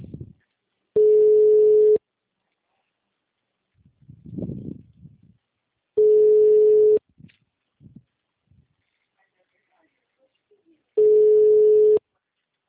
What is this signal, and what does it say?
Ringback tone (Europe, 425 Hz), with speech.